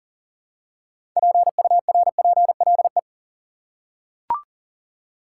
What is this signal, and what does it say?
Morse code, heard as a keyed tone, about 40 wpm, 700 Hz. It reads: PURPLE
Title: Six Letter Words - 40wpm